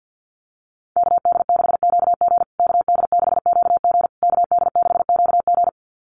Morse code, heard as a keyed tone, about 50 wpm, 700 Hz. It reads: KD6QG KD6QG KD6QG
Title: USA Call Signs Repeated 3x 50wpm